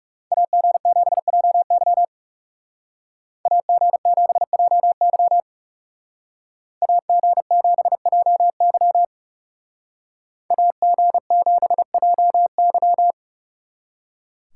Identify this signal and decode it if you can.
Morse code, heard as a keyed tone, about 37 wpm, 700 Hz. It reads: AG7JY AG7JY AG7JY AG7JY
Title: Speed Racing - USA 2x2 Call Signs 30wpm